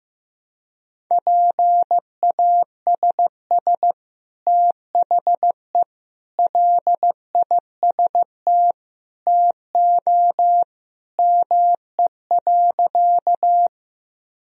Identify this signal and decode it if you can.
Morse code, heard as a keyed tone, about 15 wpm, 700 Hz. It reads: PASS THE LIST TO ME.
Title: Sentences from Top 1000 Words - Review Part 2 - 15wpm